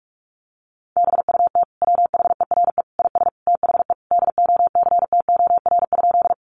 Morse code, heard as a keyed tone, about 44 wpm, 700 Hz. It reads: BUT WHERE IS THE DOCTOR?
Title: Sentences from Top 1500 Words 45wpm